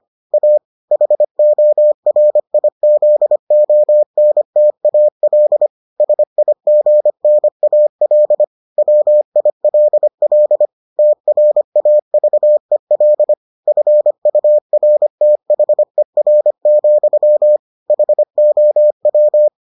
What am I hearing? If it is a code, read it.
Morse code, heard as a keyed tone, about 25 wpm, 600 Hz. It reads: A HORIZONTAL SIGNAL WILL TRAVEL FURTHER, HOW